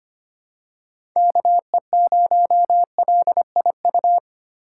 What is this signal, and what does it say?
Morse code, heard as a keyed tone, about 25 wpm, 700 Hz. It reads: KE0LIU
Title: USA Call Signs - 3x Character Spacing - 25wpm